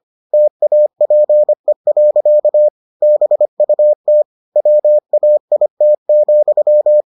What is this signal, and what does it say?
Morse code, heard as a keyed tone, about 25 wpm, 600 Hz. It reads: TAPE. BUT WAIT,